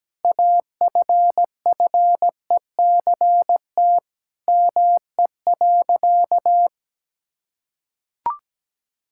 Morse code, heard as a keyed tone, about 17 wpm, 700 Hz. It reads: AFFECT ME.
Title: Sentences from Top 1000 Words - Review Part 6 - 17wpm